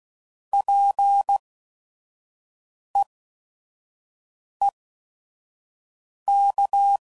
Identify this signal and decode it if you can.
Morse code, heard as a keyed tone, about 16 wpm, spaced as for 5 wpm, 800 Hz. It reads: PEEK
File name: Prefixes / By First Letter / Ps